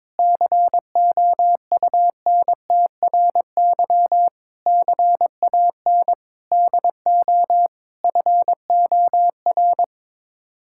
Morse code, heard as a keyed tone, about 22 wpm, 700 Hz. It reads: COUNTRY CAN DO FOR
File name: practice / 22wpm / Sets of 8 Words - 22wpm